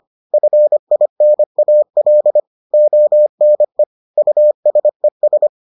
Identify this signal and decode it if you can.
Morse code, heard as a keyed tone, about 25 wpm, 600 Hz. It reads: FINAL ONE USES